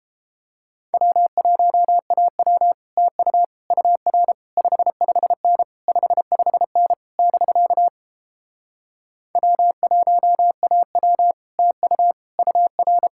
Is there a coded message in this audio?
Yes: Morse code from 0.9 to 13.1 s, heard as a keyed tone, about 33 wpm, 700 Hz. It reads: W1AW TU UR 55N 55N <BK> W1AW TU UR